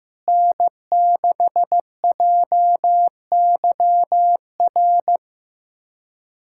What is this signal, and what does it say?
Morse code, heard as a keyed tone, about 15 wpm, 700 Hz. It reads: N6JYR
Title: USA Callsign DE Callsign 15wpm